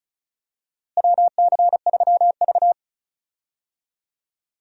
Morse code, heard as a keyed tone, about 36 wpm, 700 Hz. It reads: WC3V
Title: USA Call Signs - 4 Characters - Repeated 3x - 35wpm